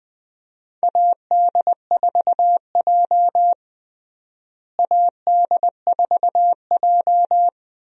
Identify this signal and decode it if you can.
Morse code, heard as a keyed tone, about 20 wpm, 700 Hz. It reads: AD4J AD4J
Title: USA Call Signs - 4 Characters - Repeated 2x - 20wpm